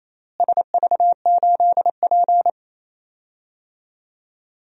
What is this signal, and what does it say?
Morse code, heard as a keyed tone, about 28 wpm, 700 Hz. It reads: SV8P
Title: DXCC-100 Call Signs - 4 Characters - Repeated 2x - 28wpm